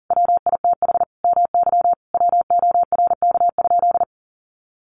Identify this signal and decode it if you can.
Morse code, heard as a keyed tone, about 40 wpm, 700 Hz. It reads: WITH MY WORK?
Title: Sentences from Top 500 Words 40wpm